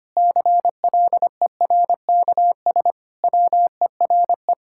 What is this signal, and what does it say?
Morse code, heard as a keyed tone, about 25 wpm, 700 Hz. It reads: CLERKS WERE
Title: Pangram Challenge v2 - 25wpm